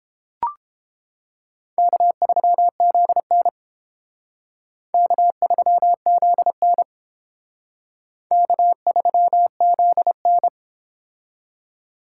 Morse code, heard as a keyed tone, about 29 wpm, 700 Hz. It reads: K3ZN K3ZN K3ZN